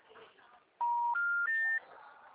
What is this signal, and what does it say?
Special information tone, with speech.